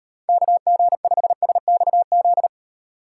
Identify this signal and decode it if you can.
Morse code, heard as a keyed tone, about 38 wpm, 700 Hz. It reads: KG5SXZ